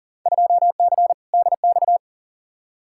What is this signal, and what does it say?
Morse code, heard as a keyed tone, about 40 wpm, 700 Hz. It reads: 2C DX
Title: Field Day Exchanges Encore - 40wpm